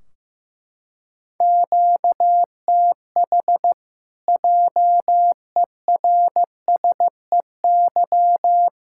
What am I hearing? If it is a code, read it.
Morse code, heard as a keyed tone, about 15 wpm, 700 Hz. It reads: QTH JERSEY
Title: Practice QSO - Exchange 1 - Beginning - 15wpm